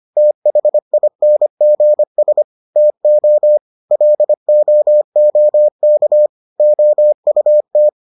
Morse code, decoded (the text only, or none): THINGS TO LOOK OUT